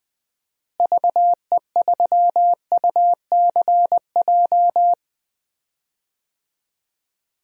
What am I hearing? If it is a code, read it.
Morse code, heard as a keyed tone, about 20 wpm, 700 Hz. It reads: VE3UCJ
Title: DXCC-100 Call Signs - 6 Characters - Repeated 3x - 20wpm